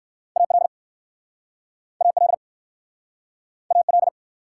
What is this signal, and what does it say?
Morse code, heard as a keyed tone, about 60 wpm, 690 Hz. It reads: AL AL AL